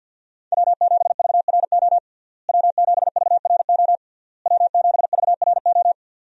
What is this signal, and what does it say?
Morse code, heard as a keyed tone, about 51 wpm, 700 Hz. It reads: W7VRO W7VRO W7VRO
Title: USA Call Signs - 5 Characters - Repeated 3x - 50wpm